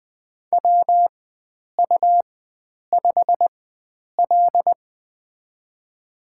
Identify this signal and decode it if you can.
Morse code, heard as a keyed tone, about 20 wpm, spaced as for 9 wpm, 700 Hz. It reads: WU5L
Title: USA Call Signs - 4x Character Spacing - 20wpm